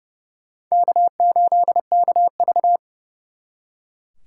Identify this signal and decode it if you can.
Morse code, heard as a keyed tone, about 39 wpm, 700 Hz. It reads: K8KV